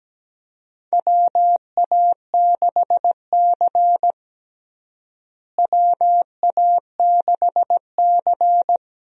Morse code, heard as a keyed tone, about 17 wpm, 700 Hz. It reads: WA6C WA6C
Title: USA Call Signs - 4 Characters - Repeated 3x - 17wpm